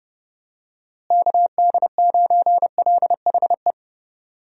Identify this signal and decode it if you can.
Morse code, heard as a keyed tone, about 30 wpm, 700 Hz. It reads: KD9LHE
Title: USA Call Signs - 6 Characters - Repeated 2x - 30wpm